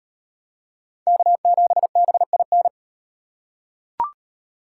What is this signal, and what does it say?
Morse code, heard as a keyed tone, about 38 wpm, 700 Hz. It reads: K7BIN